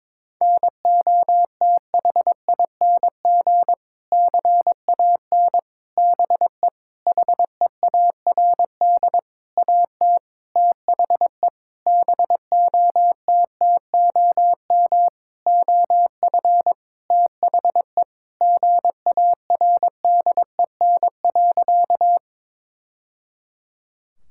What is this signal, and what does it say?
Morse code, heard as a keyed tone, about 22 wpm, 700 Hz. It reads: NOTHING CAN BE HEARD AT THE BOTTOM OF THE GARDEN.
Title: Sentences from Top 1000 Words - Review Part 6 - 22wpm